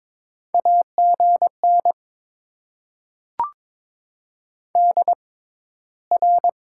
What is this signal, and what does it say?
Morse code, heard as a keyed tone, about 22 wpm, spaced as for 13 wpm, 700 Hz. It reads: AGN DR
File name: Top 90 QSO Elements - 6x Character Spacing - 22wpm